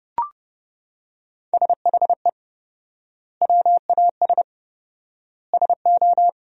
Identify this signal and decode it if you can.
Morse code, heard as a keyed tone, about 32 wpm, 700 Hz. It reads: SHE WAS SO
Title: 4x Word-Spacing - Sentences from Top 400 Words - 30wpm